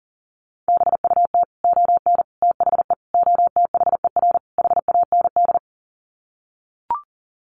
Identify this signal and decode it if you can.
Morse code, heard as a keyed tone, about 40 wpm, 700 Hz. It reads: BUT ON THE OTHER HAND
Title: Sets of 5 Words 40wpm